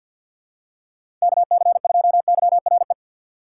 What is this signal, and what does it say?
Morse code, heard as a keyed tone, about 47 wpm, 700 Hz. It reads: KC2YRE
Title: USA Call Signs - 6 Characters - 6x Character Spacing - 50wpm